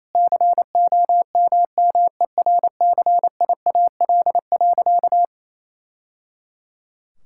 Morse code, heard as a keyed tone, about 28 wpm, 700 Hz. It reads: COMMERCIAL.
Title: Sentences from Top 1000 Words - Review Part 5 - 28wpm